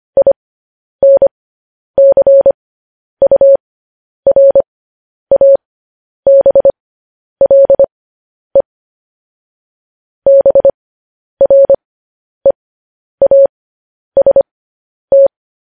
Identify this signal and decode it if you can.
Morse code, heard as a keyed tone, about 25 wpm, spaced as for 10 wpm, 550 Hz. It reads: INCURABLE BREAST
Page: news headlines in morse code at 10 wpm